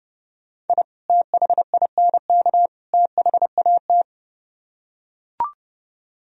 Morse code, heard as a keyed tone, about 30 wpm, 700 Hz. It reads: I THINK THAT